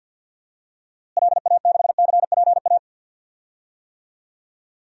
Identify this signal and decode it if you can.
Morse code, heard as a keyed tone, about 51 wpm, 700 Hz. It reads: LA6CPA
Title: DXCC-100 Call Signs - 6 Characters - Repeated 2x - 50wpm